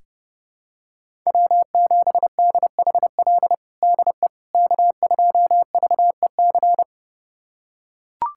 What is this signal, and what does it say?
Morse code, heard as a keyed tone, about 30 wpm, 700 Hz. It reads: W7DHL DE K2VEC
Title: USA Callsign DE Callsign - 5 Characters - 30wpm